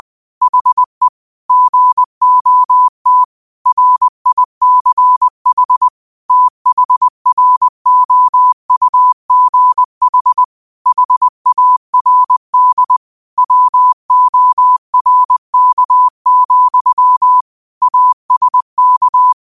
Morse code, heard as a keyed tone, about 20 wpm, 1000 Hz. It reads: HE GOT RICH THROUGH HARD WORK, ASK
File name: qotdfiles / 20wpm